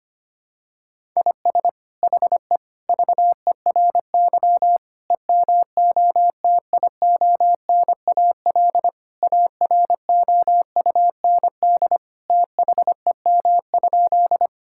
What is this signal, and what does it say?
Morse code, heard as a keyed tone, about 25 wpm, 700 Hz. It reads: IS HE VERY EMOTIONAL AROUND THEM?
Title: Sentences from Top 1000 Words - Review Part 5 - 25wpm